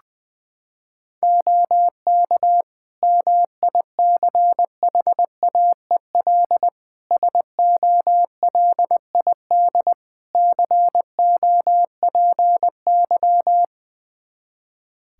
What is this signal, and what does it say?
Morse code, heard as a keyed tone, about 20 wpm, 700 Hz. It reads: OK MICHAEL SOLID COPY
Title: Practice QSO - Exchange 3 - More Info - 20wpm